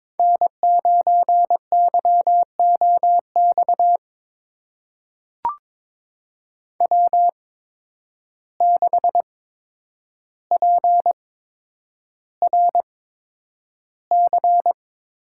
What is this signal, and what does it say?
Morse code, heard as a keyed tone, about 22 wpm, spaced as for 9 wpm, 700 Hz. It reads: N9YOX W6PRC
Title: USA Call Signs - 5 Characters - 8x Character Spacing - 22wpm